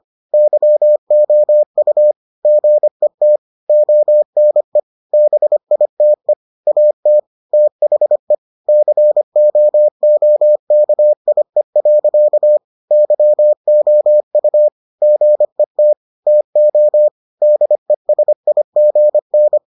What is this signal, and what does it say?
Morse code, heard as a keyed tone, about 25 wpm, 600 Hz. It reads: YOU GET ONE BITE AT THE COOKIE. YOU GET TO DESIGN